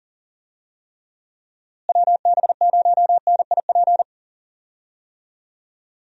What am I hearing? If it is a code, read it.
Morse code, heard as a keyed tone, about 40 wpm, 700 Hz. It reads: WB0NIP
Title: USA Call Signs - 6 Characters - Repeated 2x - 40wpm